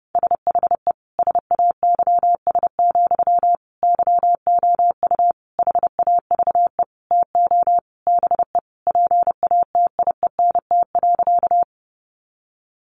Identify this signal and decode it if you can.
Morse code, heard as a keyed tone, about 28 wpm, 700 Hz. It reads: SHE SAYS, YOU HAVE TO BE PATIENT.
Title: Easy Phrases 30wpm